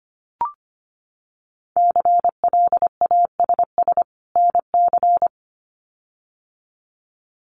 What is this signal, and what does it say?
Morse code, heard as a keyed tone, about 25 wpm, 700 Hz. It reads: CLASS NC
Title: Winter Field Day Exchanges 25wpm